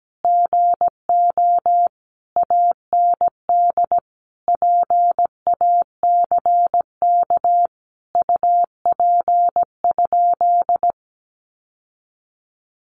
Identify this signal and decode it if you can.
Morse code, heard as a keyed tone, about 17 wpm, 700 Hz. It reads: GO AND PACK UP?
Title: Sentences from Top 1300 Words 17wpm